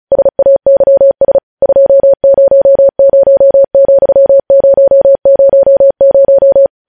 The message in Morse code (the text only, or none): SAYS 200,000